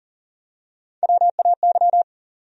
Morse code, heard as a keyed tone, about 40 wpm, 700 Hz. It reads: WAY
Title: Top 100 Words 40wpm